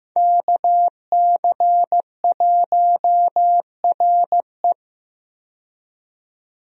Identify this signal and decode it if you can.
Morse code, heard as a keyed tone, about 15 wpm, 700 Hz. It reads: KC1RE
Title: USA Call Signs - 5 Characters - 15wpm